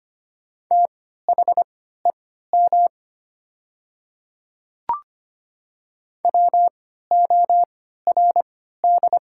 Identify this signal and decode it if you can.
Morse code, heard as a keyed tone, about 25 wpm, spaced as for 14 wpm, 690 Hz. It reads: THEM WORD
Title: 3x Character Spacing - Top 100 Words - 25wpm